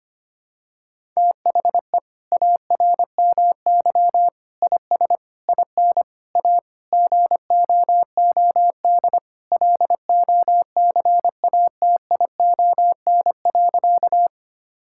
Morse code, heard as a keyed tone, about 25 wpm, 700 Hz. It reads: THE ARMY IS IN A GOOD LOCATION.
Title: Sentences from Top 1000 Words - Review Part 6 - 25wpm